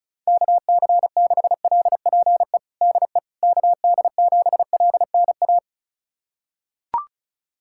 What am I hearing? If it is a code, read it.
Morse code, heard as a keyed tone, about 34 wpm, 700 Hz. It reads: KC6LPE DE KD7LNA
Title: USA Callsign DE Callsign - 6 Characters - 35wpm